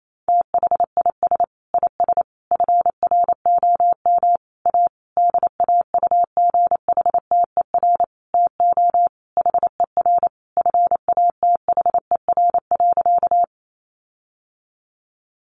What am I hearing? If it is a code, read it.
Morse code, heard as a keyed tone, about 28 wpm, 700 Hz. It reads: THIS IS FROM A DAUGHTER TO HER FATHER.
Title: Sentences from Top 1500 Words 28wpm